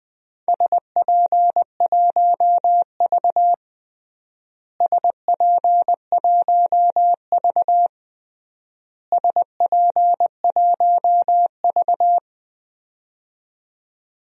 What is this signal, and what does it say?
Morse code, heard as a keyed tone, about 21 wpm, 700 Hz. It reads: SP1V SP1V SP1V
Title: DXCC-100 Call Signs - 4 Characters - Repeated 3x - 20wpm